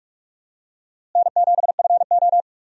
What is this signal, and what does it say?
Morse code, heard as a keyed tone, about 42 wpm, 700 Hz. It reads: N7FO
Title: USA Call Signs - 4 Characters - Repeated 2x - 45wpm